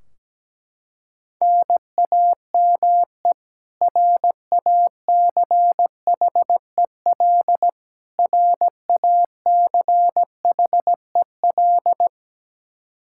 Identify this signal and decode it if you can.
Morse code, heard as a keyed tone, about 17 wpm, 700 Hz. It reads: NAME RACHEL RACHEL
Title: Practice QSO - Exchange 1 - Beginning - 17wpm